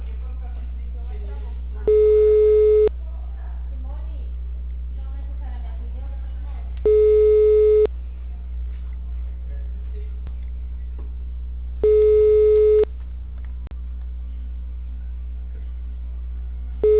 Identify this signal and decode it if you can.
Ringback tone (Europe, 425 Hz), with speech.